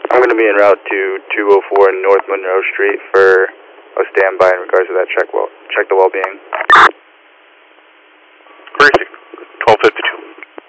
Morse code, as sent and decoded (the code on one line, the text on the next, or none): none